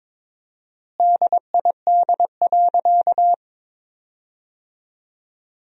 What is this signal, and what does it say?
Morse code, heard as a keyed tone, about 21 wpm, 700 Hz. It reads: DID.
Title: 4x Word-Spacing - Sentences from Top 100 Words - 22wpm